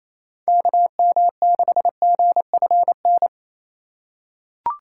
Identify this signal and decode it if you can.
Morse code, heard as a keyed tone, about 28 wpm, 700 Hz. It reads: KM6GFN
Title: USA Call Signs - 6 Characters - 2x Character Spacing - 28wpm